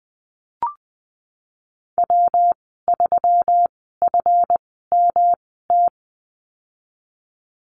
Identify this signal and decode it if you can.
Morse code, heard as a keyed tone, about 20 wpm, spaced as for 14 wpm, 700 Hz. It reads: W3FMT
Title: 147 Lesson 24g - Letter y - Call Signs 20wpm